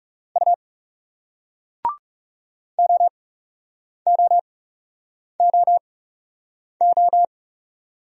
Morse code, heard as a keyed tone, about 39 wpm, spaced as for 8 wpm, 700 Hz. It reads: U OOOO